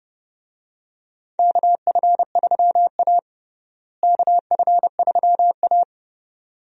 Morse code, heard as a keyed tone, about 30 wpm, 700 Hz. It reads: KF3A KF3A
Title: USA Call Signs - 4 Characters - Repeated 2x - 30wpm